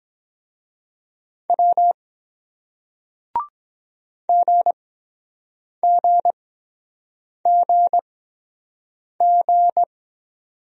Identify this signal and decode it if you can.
Morse code, heard as a keyed tone, about 23 wpm, spaced as for 7 wpm, 700 Hz. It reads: W GGGG